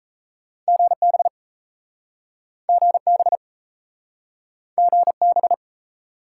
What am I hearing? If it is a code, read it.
Morse code, heard as a keyed tone, about 37 wpm, 700 Hz. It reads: GB GB GB